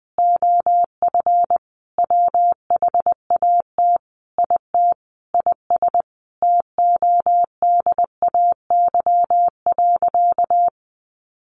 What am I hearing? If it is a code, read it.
Morse code, heard as a keyed tone, about 20 wpm, 700 Hz. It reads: OF WHAT IT IS TODAY.
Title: Sentences from Top 1500 Words 20wpm